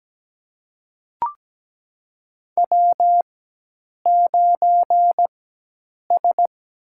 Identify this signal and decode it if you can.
Morse code, heard as a keyed tone, about 17 wpm, spaced as for 8 wpm, 700 Hz. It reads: W9S
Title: USA Call Signs - 4x Character Spacing - 17wpm